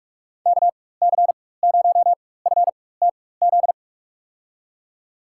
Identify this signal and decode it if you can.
Morse code, heard as a keyed tone, about 44 wpm, spaced as for 21 wpm, 700 Hz. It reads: KC0FTZ
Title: USA Call Signs - 6 Characters - 4x Character Spacing - 45wpm